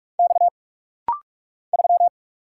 Morse code, heard as a keyed tone, about 46 wpm, 700 Hz. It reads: X 3
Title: Single Letter-Number Alt - Rapid-Fire 45wpm